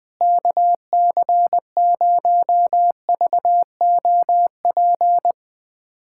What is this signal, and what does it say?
Morse code, heard as a keyed tone, about 21 wpm, 700 Hz. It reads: KC0VOP
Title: USA Callsign DE Callsign 20wpm